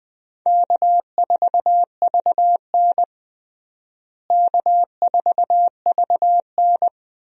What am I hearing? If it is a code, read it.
Morse code, heard as a keyed tone, about 20 wpm, 700 Hz. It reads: K4VN K4VN
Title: USA Call Signs - 4 Characters - Repeated 2x - 20wpm